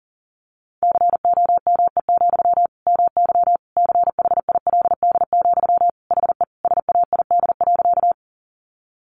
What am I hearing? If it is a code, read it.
Morse code, heard as a keyed tone, about 40 wpm, 700 Hz. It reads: COME, MY CHILD, HE SAID.
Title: Sentences from Top 800 Words 40wpm